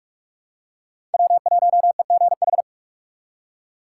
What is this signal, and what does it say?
Morse code, heard as a keyed tone, about 46 wpm, 700 Hz. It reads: W1EGH
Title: USA Call Signs - 2x Character Spacing - 45wpm